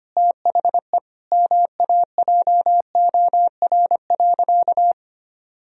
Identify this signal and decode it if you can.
Morse code, heard as a keyed tone, about 25 wpm, 700 Hz. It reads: THE MAJOR.
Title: Sentences from Top 1000 Words - Review Part 4 - 25wpm